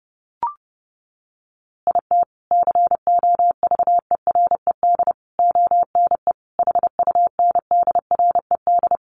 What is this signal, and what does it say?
Morse code, heard as a keyed tone, about 30 wpm, 700 Hz. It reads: IT COVERED ONE HUNDRED
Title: A Sound Of Thunder 30wpm